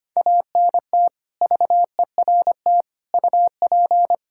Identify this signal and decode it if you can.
Morse code, heard as a keyed tone, about 25 wpm, 700 Hz. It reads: ANT VERT UP
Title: Practice QSO - Exchange 2 - Info - 25wpm